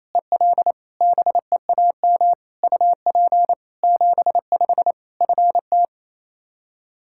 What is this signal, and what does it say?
Morse code, heard as a keyed tone, about 28 wpm, 700 Hz. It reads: EL BEAM UP 75 FT
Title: Practice QSO - Exchange 2 - Info - 28wpm